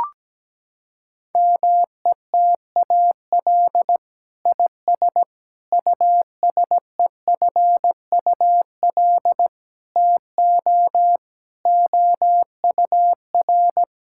Morse code, heard as a keyed tone, about 17 wpm, 700 Hz. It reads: METAL IS USEFUL TO OUR
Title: Sentences from Top 1000 Words - Review Part 6 - 17wpm